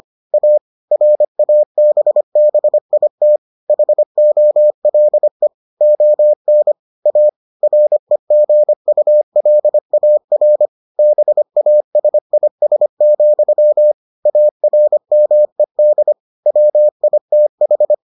Morse code, heard as a keyed tone, about 25 wpm, 600 Hz. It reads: A RABBIT HOLE ON A REGULAR BASIS, ARMED WITH